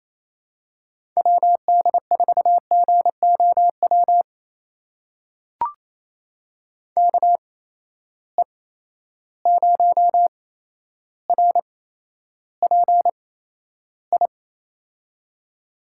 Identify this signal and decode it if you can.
Morse code, heard as a keyed tone, about 28 wpm, spaced as for 12 wpm, 700 Hz. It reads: WD4GOW KE0RPI
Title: USA Call Signs - 6 Characters - 8x Character Spacing - 28wpm